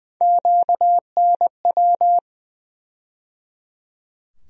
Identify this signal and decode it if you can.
Morse code, heard as a keyed tone, about 20 wpm, 700 Hz. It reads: QNW
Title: USA Call Sign Postfixes - 3 Letters 20wpm